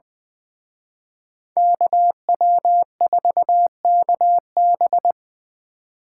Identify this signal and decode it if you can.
Morse code, heard as a keyed tone, about 24 wpm, 700 Hz. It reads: KW4KB
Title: Speed Racing - USA Call Signs 20wpm